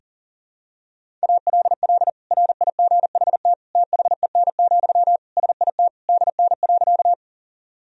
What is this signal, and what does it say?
Morse code, heard as a keyed tone, about 37 wpm, 700 Hz. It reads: ALL RIGHT THEN, SIT DN.
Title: Sentences from Top 500 Words and Abbreviations 40wpm